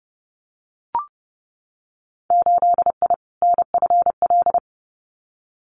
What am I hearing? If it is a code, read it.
Morse code, heard as a keyed tone, about 32 wpm, 700 Hz. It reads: 8I NFL
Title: Winter Field Day Exchanges Encore 30wpm